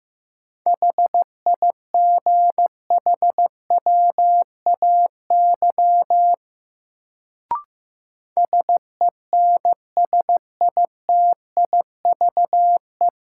Morse code, heard as a keyed tone, about 15 wpm, 700 Hz. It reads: HIGHWAY SENSITIVE
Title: Sentences from Top 1400 Words - New Word Drill - 15wpm